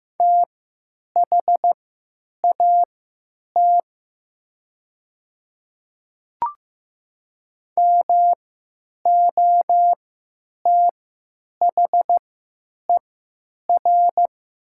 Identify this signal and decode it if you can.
Morse code, heard as a keyed tone, about 15 wpm, spaced as for 9 wpm, 700 Hz. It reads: THAT MOTHER
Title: 3x Character Spacing - Top 300 Words - 15wpm